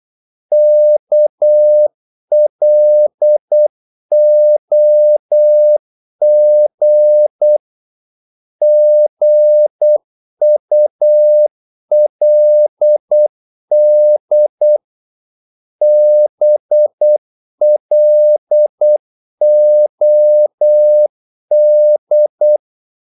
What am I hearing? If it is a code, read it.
Morse code, heard as a keyed tone, about 8 wpm, 600 Hz. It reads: KLOG GULD BLOD
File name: Korte ord DK 8wpm